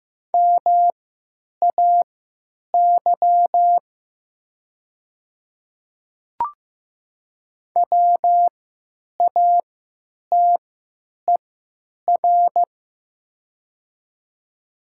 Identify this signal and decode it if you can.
Morse code, heard as a keyed tone, about 15 wpm, spaced as for 9 wpm, 700 Hz. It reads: MAY WATER
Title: 3x Character Spacing - Top 100 Words - 15wpm